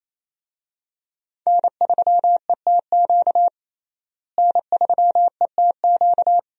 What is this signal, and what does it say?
Morse code, heard as a keyed tone, about 28 wpm, 700 Hz. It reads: N3ETQ N3ETQ
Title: USA Call Signs - 5 Characters - Repeated 2x - 28wpm